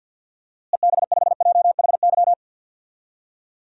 Morse code, heard as a keyed tone, about 51 wpm, 700 Hz. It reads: EB5JHY
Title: DXCC-100 Call Signs - 6 Characters - Repeated 3x - 50wpm